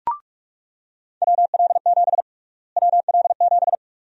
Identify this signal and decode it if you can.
Morse code, heard as a keyed tone, about 46 wpm, 700 Hz. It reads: WL7 WL7
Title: USA Call Sign Prefixes - Repeated 2x - 45wpm